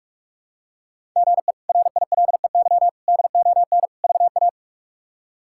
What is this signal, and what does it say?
Morse code, heard as a keyed tone, about 46 wpm, 700 Hz. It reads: GE RILEY DON VA
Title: K1USN SST QSO Practice - 45wpm